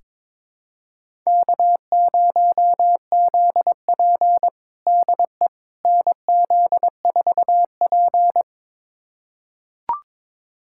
Morse code, heard as a keyed tone, about 22 wpm, 700 Hz. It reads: K0ZP DE NZ4P
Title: USA Callsign DE Callsign - 4 Characters - 22wpm